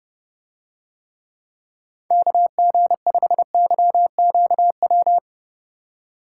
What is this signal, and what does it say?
Morse code, heard as a keyed tone, about 29 wpm, 700 Hz. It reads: KG5YQW